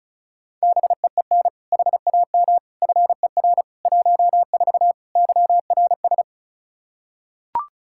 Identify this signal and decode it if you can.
Morse code, heard as a keyed tone, about 34 wpm, 700 Hz. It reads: BEEN HAM FER 14 YRS
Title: Practice QSO - Exchange 3 - More Info - 35wpm